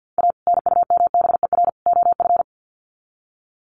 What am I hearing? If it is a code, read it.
Morse code, heard as a keyed tone, about 50 wpm, 700 Hz. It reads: A NUMBER OF
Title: Sets of 3 Words 50wpm